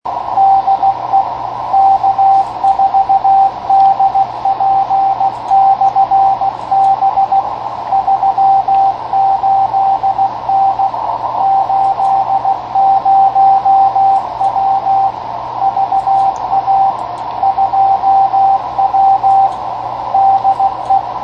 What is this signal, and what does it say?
Morse code, heard as a keyed tone, about 16 wpm, 780 Hz, with radio noise. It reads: DE K4DP/B 38N 80W VA 2W DE
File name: 10m/K4DP 28.234 2watt